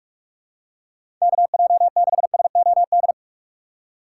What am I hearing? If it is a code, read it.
Morse code, heard as a keyed tone, about 44 wpm, 700 Hz. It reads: KJ6SOD
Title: USA Call Signs - 6 Characters - 4x Character Spacing - 45wpm